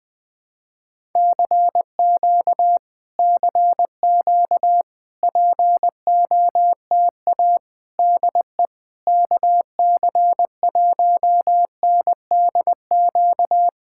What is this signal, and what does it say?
Morse code, heard as a keyed tone, about 20 wpm, 700 Hz. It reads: CQ CQ POTA DE KC1NDQ
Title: POTA QSO Practice - 20wpm